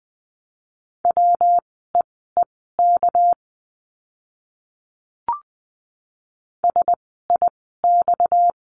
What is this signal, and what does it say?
Morse code, heard as a keyed tone, about 20 wpm, spaced as for 15 wpm, 700 Hz. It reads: WEEK SIX
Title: 230 Lesson 37f - Letter x - Words 20wpm